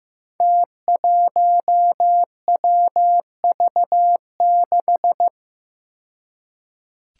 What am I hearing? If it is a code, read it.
Morse code, heard as a keyed tone, about 15 wpm, 700 Hz. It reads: T1WV6
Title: Five Challenging Letter-Number Combinations 15wpm